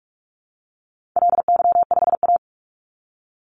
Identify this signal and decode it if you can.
Morse code, heard as a keyed tone, about 44 wpm, 700 Hz. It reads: LY5A
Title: Call Signs Repeated 2x - 45wpm